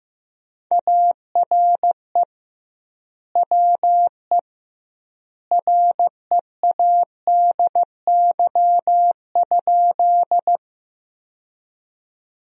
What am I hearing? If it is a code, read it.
Morse code, heard as a keyed tone, about 15 wpm, 700 Hz. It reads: ARE WE READY?
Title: 2x Word-Spacing - Sentences from Top 400 Words - 15wpm